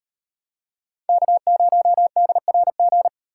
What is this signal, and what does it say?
Morse code, heard as a keyed tone, about 36 wpm, 700 Hz. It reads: K0DRG